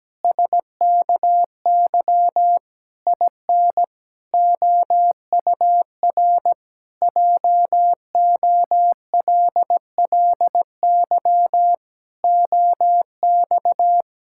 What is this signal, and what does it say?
Morse code, heard as a keyed tone, about 17 wpm, 700 Hz. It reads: SKY IN OUR JOLLY OX
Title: Pangram Challenge v2 - 17wpm